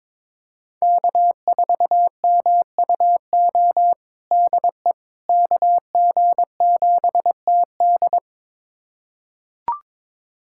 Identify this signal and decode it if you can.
Morse code, heard as a keyed tone, about 22 wpm, 700 Hz. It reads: K4MUO DE KG7TD
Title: USA Callsign DE Callsign - 5 Characters - 22wpm